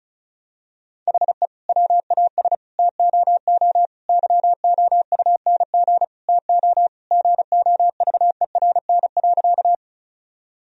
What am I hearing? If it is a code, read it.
Morse code, heard as a keyed tone, about 36 wpm, 700 Hz. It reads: HE WAS TOO YOUNG TO GOVERN.
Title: Sentences from Top 1000 Words - Review Part 2 - 35wpm